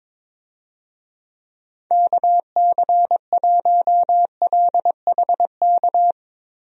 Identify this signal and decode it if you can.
Morse code, heard as a keyed tone, about 21 wpm, 700 Hz. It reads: KC1LHK
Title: USA Call Signs - 6 Characters - Repeated 3x - 22wpm